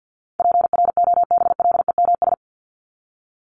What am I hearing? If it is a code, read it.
Morse code, heard as a keyed tone, about 50 wpm, 700 Hz. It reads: PROBLEMS
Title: ES:S Suffix - Top 100 Words with Base Words 50wpm